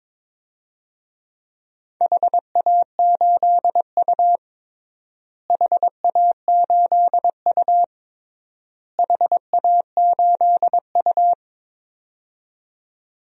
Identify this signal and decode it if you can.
Morse code, heard as a keyed tone, about 21 wpm, 700 Hz. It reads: HA8U HA8U HA8U
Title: DXCC-100 Call Signs - 4 Characters - Repeated 3x - 22wpm